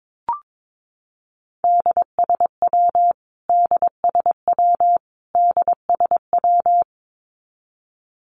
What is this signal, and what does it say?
Morse code, heard as a keyed tone, about 22 wpm, 700 Hz. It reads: DSW DSW DSW
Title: Abbreviations Repeated 3x - 22wpm